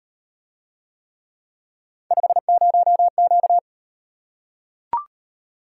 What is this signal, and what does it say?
Morse code, heard as a keyed tone, about 38 wpm, 700 Hz. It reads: 50Q